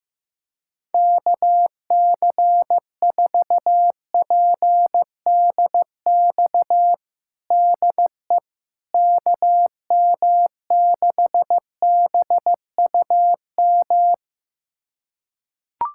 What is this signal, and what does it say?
Morse code, heard as a keyed tone, about 15 wpm, 700 Hz. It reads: KC4PDX DE KM6BUM
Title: USA Callsign DE Callsign 15wpm